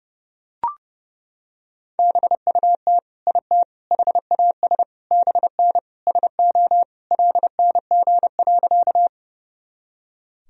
Morse code, heard as a keyed tone, about 30 wpm, 700 Hz. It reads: BUT IT HAS BN SO LNG.
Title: Sentences from Top 100 Words and Abbreviations 30wpm